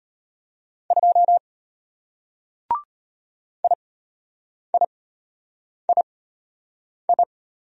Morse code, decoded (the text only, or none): 2 IIII